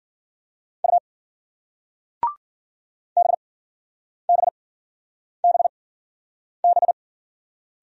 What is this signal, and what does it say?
Morse code, heard as a keyed tone, about 47 wpm, spaced as for 9 wpm, 700 Hz. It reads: U BBBB
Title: Speed-Racing Single Letters 40wpm